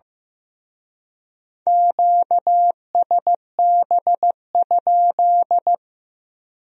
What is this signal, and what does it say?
Morse code, heard as a keyed tone, about 18 wpm, 700 Hz. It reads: QSB?
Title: Speed-Racing Q-Codes - 15wpm